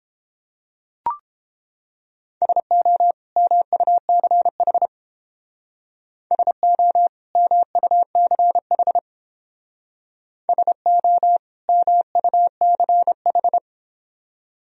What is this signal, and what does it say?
Morse code, heard as a keyed tone, about 30 wpm, 700 Hz. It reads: SO MUCH SO MUCH SO MUCH